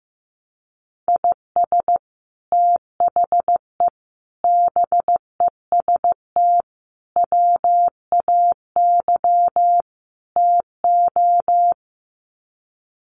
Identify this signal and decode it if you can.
Morse code, heard as a keyed tone, about 15 wpm, 700 Hz. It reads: IS THE BEST WAY TO
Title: Sets of 5 Words Encore 15wpm